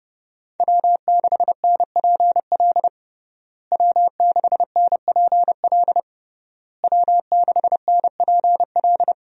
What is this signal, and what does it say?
Morse code, heard as a keyed tone, about 30 wpm, 700 Hz. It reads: W6NPL W6NPL W6NPL
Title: USA Call Signs - 5 Characters - Repeated 3x - 30wpm